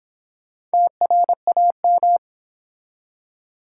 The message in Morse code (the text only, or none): TRAM